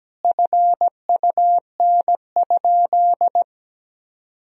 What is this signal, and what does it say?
Morse code, heard as a keyed tone, about 17 wpm, 700 Hz. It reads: FUN?
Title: Sentences from Top 1000 Words - Review Part 3 - 17wpm